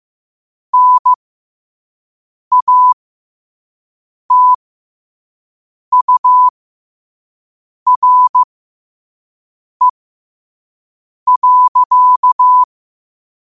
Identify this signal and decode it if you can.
Morse code, heard as a keyed tone, about 15 wpm, spaced as for 5 wpm, 1000 Hz. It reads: NATURE.
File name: qotdfiles / 5wpm